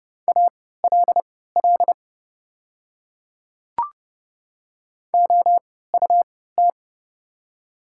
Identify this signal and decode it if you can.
Morse code, heard as a keyed tone, about 30 wpm, spaced as for 17 wpm, 700 Hz. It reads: ALL OUT
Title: 3x Character Spacing - Top 100 Words - 30wpm